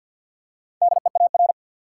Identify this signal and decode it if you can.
Morse code, heard as a keyed tone, about 49 wpm, 700 Hz. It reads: DEAR
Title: Four Letter Words - 50wpm